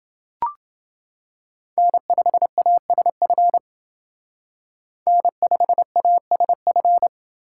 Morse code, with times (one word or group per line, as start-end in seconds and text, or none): 1.8-3.6 N5ASF
5.1-7.1 N5ASF